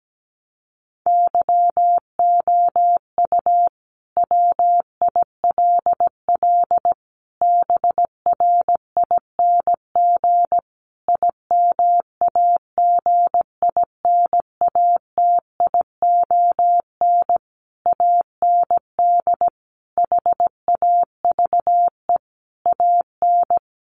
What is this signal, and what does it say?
Morse code, heard as a keyed tone, about 17 wpm, 700 Hz. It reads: YOU WILL BRING IMAGINATION AND HAVE AN
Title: Sentences from Top 1300 Words 17wpm